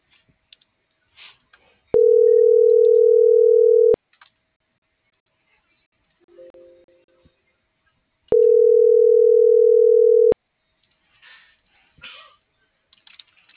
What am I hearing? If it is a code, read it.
Ringback tone (North America), 440 + 480 Hz.